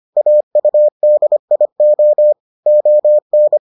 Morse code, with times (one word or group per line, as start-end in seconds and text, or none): 0.2-2.3 AUDIO
2.7-3.6 ON